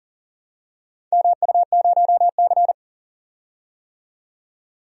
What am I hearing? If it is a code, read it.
Morse code, heard as a keyed tone, about 40 wpm, 700 Hz. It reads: MU0C
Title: DXCC-100 Call Signs - 4 Characters - 40wpm